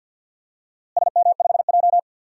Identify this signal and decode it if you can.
Morse code, heard as a keyed tone, about 50 wpm, 700 Hz. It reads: SM5J
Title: DXCC-100 Call Signs - 4 Characters - Repeated 3x - 50wpm